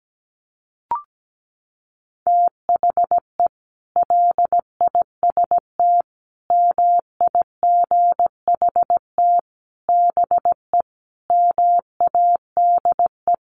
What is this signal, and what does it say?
Morse code, heard as a keyed tone, about 17 wpm, 700 Hz. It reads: THE LIST MIGHT BE MADE
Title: Sentences from Top 1100 Words 17wpm